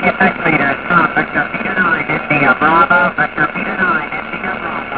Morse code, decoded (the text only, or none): none